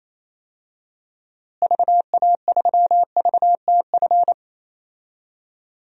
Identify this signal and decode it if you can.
Morse code, heard as a keyed tone, about 27 wpm, 700 Hz. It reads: VA3VTF
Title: DXCC-100 Call Signs - 6 Characters - Repeated 2x - 28wpm